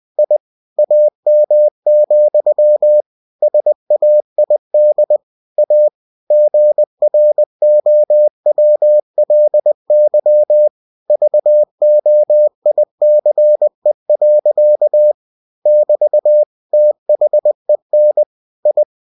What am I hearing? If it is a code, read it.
Morse code, heard as a keyed tone, about 20 wpm, 600 Hz. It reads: I AM, SAID A GROWLY VOICE. <BT> THEN I
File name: cw-books / winnie-the-pooh / winnie-20wpm-page-0002